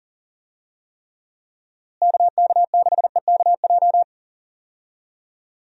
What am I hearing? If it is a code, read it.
Morse code, heard as a keyed tone, about 40 wpm, 700 Hz. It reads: KK6EKJ
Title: USA Call Signs - 6 Characters - Repeated 3x - 40wpm